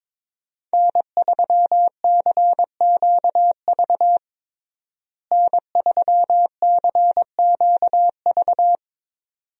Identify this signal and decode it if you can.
Morse code, heard as a keyed tone, about 22 wpm, 700 Hz. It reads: N3CQV N3CQV
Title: USA Call Signs - 5 Characters - Repeated 2x - 22wpm